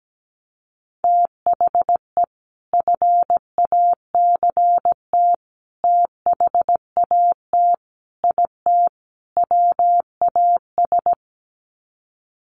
Morse code, heard as a keyed tone, about 17 wpm, 700 Hz. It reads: THE FACT THAT IT WAS
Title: Sets of 5 Words Encore2 17wpm